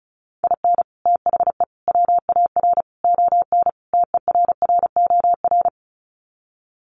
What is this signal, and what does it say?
Morse code, heard as a keyed tone, about 34 wpm, 700 Hz. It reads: IN THE WAR ON TERROR
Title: Sets of 5 Words Encore2 35wpm